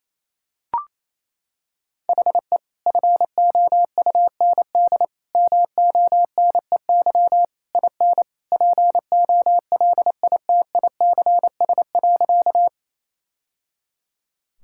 Morse code, heard as a keyed tone, about 28 wpm, 700 Hz. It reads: HE FOUND MONEY IN POLITICS.
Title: Sentences from Top 1000 Words - Review Part 6 - 28wpm